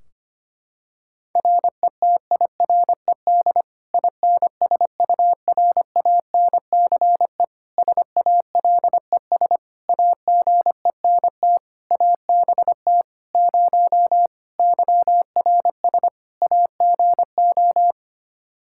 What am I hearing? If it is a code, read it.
Morse code, heard as a keyed tone, about 25 wpm, 700 Hz. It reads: RETIRED INSURANCE SALES AGENT ABT 0 YRS AGO
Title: Practice QSO - Exchange 3 - More Info - 25wpm